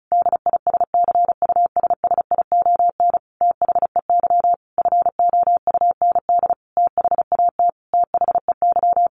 Morse code, heard as a keyed tone, about 34 wpm, 700 Hz. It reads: DISCUSSION THEY FOUND THAT THEY
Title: Sentences from Top 1200 Words 35wpm